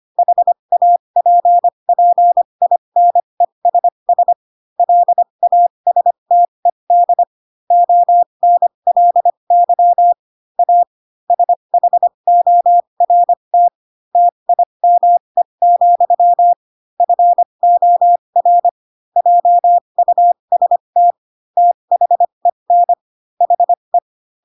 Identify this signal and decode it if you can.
Morse code, heard as a keyed tone, about 25 wpm, spaced as for 22 wpm, 700 Hz. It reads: HAPPINESS LASTED ONLY A SHORT TIME, FOR JUST THEN HE
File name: Pinocchio / Pinocchio ch4 18 25 ik1hge